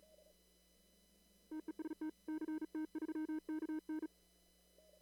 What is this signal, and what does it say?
Morse code, heard as a keyed tone, about 35 wpm, 310 Hz. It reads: TEST CT3KN
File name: CW / kt5j wpx cw-op-k5pi